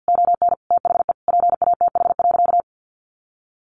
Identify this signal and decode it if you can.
Morse code, heard as a keyed tone, about 50 wpm, 700 Hz. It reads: ON THE PATH.
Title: A Sound Of Thunder 50wpm